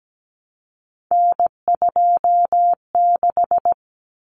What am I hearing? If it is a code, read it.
Morse code, heard as a keyed tone, about 17 wpm, 700 Hz. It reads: N26
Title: Common Cut Numbers - 3-Digit 17wpm